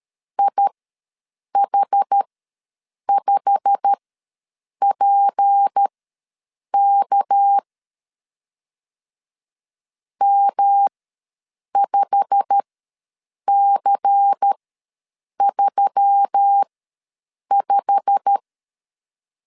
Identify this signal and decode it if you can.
Morse code, heard as a keyed tone, about 13 wpm, spaced as for 7 wpm, 800 Hz. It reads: IH5PK M5C35
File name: LEZIONI / 8 YJ45 / 8 YJ45 ws 12wpm 3146